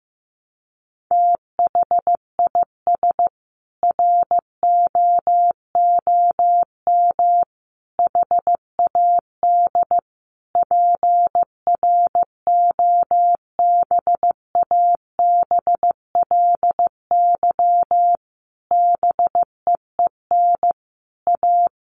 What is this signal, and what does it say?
Morse code, heard as a keyed tone, about 15 wpm, 700 Hz. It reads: THIS ROOM HAD PROBABLY BEEN A
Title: Sentences from Top 700 Words 15wpm